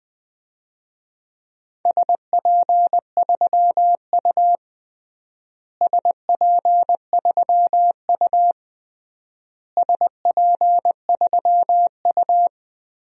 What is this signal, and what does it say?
Morse code, heard as a keyed tone, about 20 wpm, 700 Hz. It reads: SP3U SP3U SP3U
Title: DXCC-100 Call Signs - 4 Characters - Repeated 3x - 20wpm